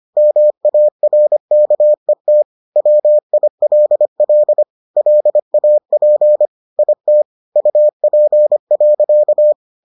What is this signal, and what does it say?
Morse code, heard as a keyed tone, about 25 wpm, 600 Hz. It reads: MARKET WILL LAP IT UP.